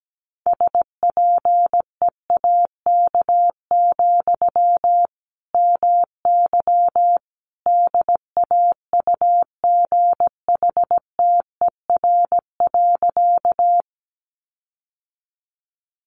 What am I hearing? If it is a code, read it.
Morse code, heard as a keyed tone, about 17 wpm, 700 Hz. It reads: SPEAK, MY DAUGHTER.
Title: Sentences from Top 1500 Words 17wpm